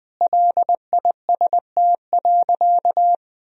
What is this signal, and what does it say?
Morse code, heard as a keyed tone, about 20 wpm, 700 Hz. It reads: LIST.
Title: Sentences from Top 1000 Words - Review Part 2 - 20wpm